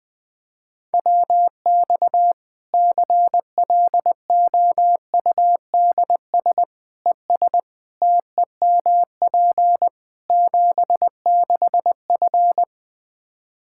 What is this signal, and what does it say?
Morse code, heard as a keyed tone, about 20 wpm, 700 Hz. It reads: WX CLOUDS ES TEMP 76F
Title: K1USN SST QSO Practice 20wpm